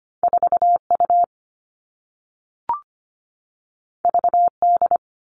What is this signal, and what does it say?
Morse code, heard as a keyed tone, about 25 wpm, 700 Hz. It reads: 4 U V D
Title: All Cut Numbers - 2-Digit 25wpm